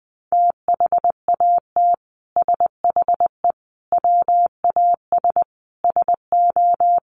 Morse code, heard as a keyed tone, about 20 wpm, 700 Hz. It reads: THAT SHE WAS SO
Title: Sentences from Top 1400 Words 20wpm